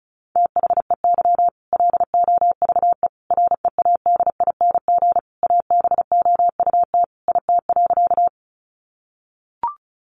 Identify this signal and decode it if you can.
Morse code, heard as a keyed tone, about 34 wpm, 700 Hz. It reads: THEY LOVE READING ABOUT IT.
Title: Sentences from Top 600 Words 35wpm